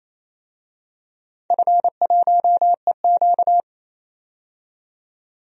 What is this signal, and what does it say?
Morse code, heard as a keyed tone, about 28 wpm, 700 Hz. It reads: F1EQ
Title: DXCC-100 Call Signs - 4 Characters - 28wpm